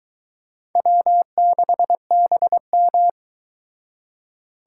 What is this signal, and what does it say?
Morse code, heard as a keyed tone, about 22 wpm, 700 Hz. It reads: W6BM